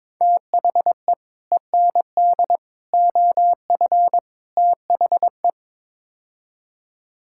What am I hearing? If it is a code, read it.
Morse code, heard as a keyed tone, about 22 wpm, 700 Hz. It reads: THE END OF THE
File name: practice / 22wpm / Build A Phrase - 5-Word Phrases - 22wpm